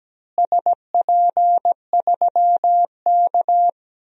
Morse code, heard as a keyed tone, about 17 wpm, 700 Hz. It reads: SP3K
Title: DXCC-100 Call Signs - 4 Characters - Repeated 3x - 17wpm